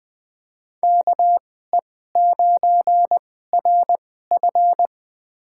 Morse code, heard as a keyed tone, about 20 wpm, spaced as for 14 wpm, 700 Hz. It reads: KE9RF
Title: USA Call Signs - 5 Characters - 2x Character Spacing - 20wpm